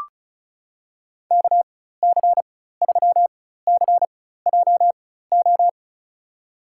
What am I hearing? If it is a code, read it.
Morse code, heard as a keyed tone, about 36 wpm, spaced as for 16 wpm, 700 Hz. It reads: KC3CJO
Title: USA Call Signs - 6 Characters - 4x Character Spacing - 35wpm